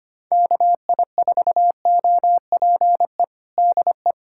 Morse code, heard as a keyed tone, about 25 wpm, 700 Hz. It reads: KI4OPE DE
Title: USA Callsign DE Callsign 25wpm